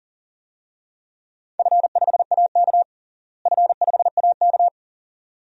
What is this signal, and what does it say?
Morse code, heard as a keyed tone, about 40 wpm, 700 Hz. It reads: F5AK F5AK
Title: DXCC-100 Call Signs - 4 Characters - Repeated 2x - 40wpm